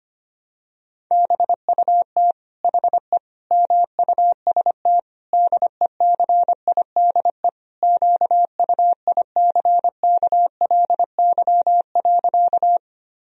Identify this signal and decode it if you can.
Morse code, heard as a keyed tone, about 25 wpm, 700 Hz. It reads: BUT HE MUST DECIDE QUICKLY.
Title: Sentences from Top 1000 Words - Review Part 3 - 25wpm